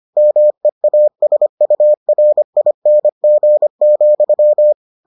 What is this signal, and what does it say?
Morse code, heard as a keyed tone, about 25 wpm, 600 Hz. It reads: MEASURING,